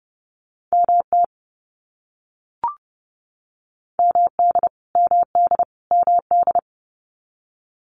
Morse code, heard as a keyed tone, about 29 wpm, 700 Hz. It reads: MT MD MD MD
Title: US States Repeated 3x 30wpm